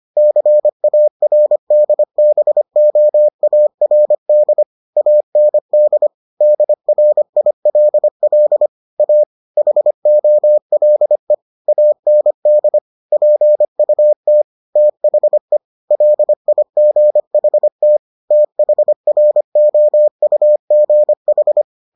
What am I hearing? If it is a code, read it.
Morse code, heard as a keyed tone, about 25 wpm, 600 Hz. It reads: CARDBOARD AND DRILL A HOLE AND PUT THE LIGHT THROUGH